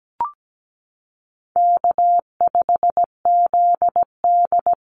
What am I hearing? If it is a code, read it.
Morse code, heard as a keyed tone, about 17 wpm, 700 Hz. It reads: K5ZD
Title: Easy Call Signs - 17wpm